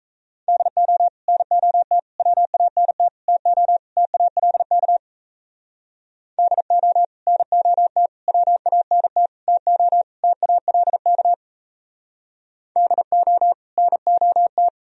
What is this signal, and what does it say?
Morse code, heard as a keyed tone, about 36 wpm, 700 Hz. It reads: DO NOT WANT TO TALK DO NOT WANT TO TALK DO NOT